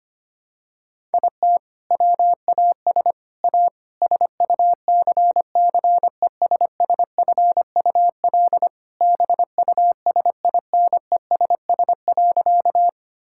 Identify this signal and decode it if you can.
Morse code, heard as a keyed tone, about 25 wpm, 700 Hz. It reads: IT WAS A SUCCESSFUL BUSINESS.
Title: Sentences from Top 1000 Words - Review Part 6 - 25wpm